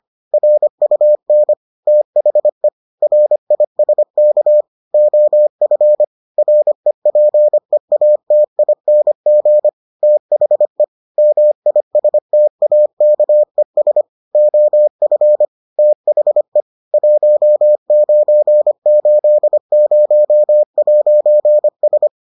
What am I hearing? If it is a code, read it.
Morse code, heard as a keyed tone, about 25 wpm, 600 Hz. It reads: RUN THE RISK OF REPEATING THE MISTAKES OF THE 1980'S